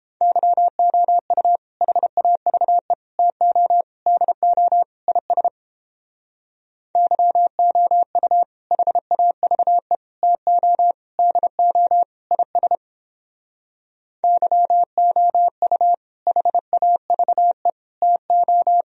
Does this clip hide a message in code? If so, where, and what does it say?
Yes: Morse code from 0.2 to 18.8 s, heard as a keyed tone, about 31 wpm, 700 Hz. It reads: YOU HAVE TO DO IS YOU HAVE TO DO IS YOU HAVE TO